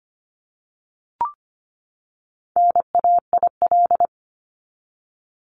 Morse code, heard as a keyed tone, about 25 wpm, 700 Hz. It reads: NAIL
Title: Top 1501-1750 Words 25wpm